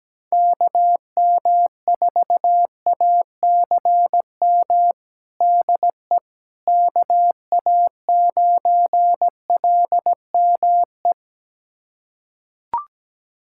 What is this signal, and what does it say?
Morse code, heard as a keyed tone, about 17 wpm, 700 Hz. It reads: KM4ACM DE KA9LME
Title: USA Callsign DE Callsign 17wpm